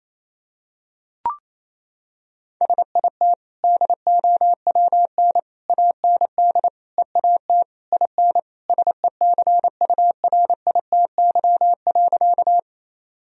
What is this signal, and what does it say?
Morse code, heard as a keyed tone, about 28 wpm, 700 Hz. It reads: SIT DOWN AND EAT IN SECURITY.
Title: Sentences from Top 1000 Words - Review Part 4 - 28wpm